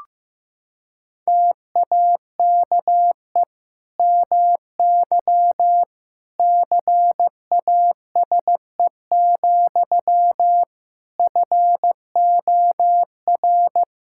Morse code, heard as a keyed tone, about 16 wpm, 700 Hz. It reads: TAKE MY CASE, FOR
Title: Sentences from Top 1000 Words - Review Part 5 - 15wpm